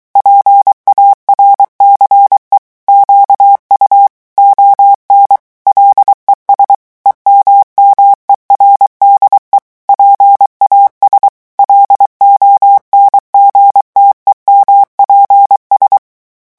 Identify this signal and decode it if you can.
Morse code, heard as a keyed tone, about 23 wpm, 800 Hz. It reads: PARCE QU ON LES EMMERDE PAS LONGTEMPS
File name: morse mp3 / revision9 citations humoristiques vitesse 22 mots minute